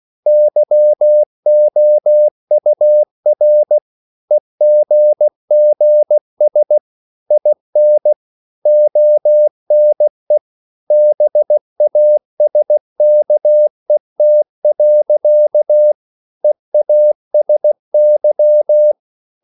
Morse code, heard as a keyed tone, about 16 wpm, 600 Hz. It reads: YOUR EGGS IN ONE BASKET. EASY
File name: CW / Practice / EnglishSayings16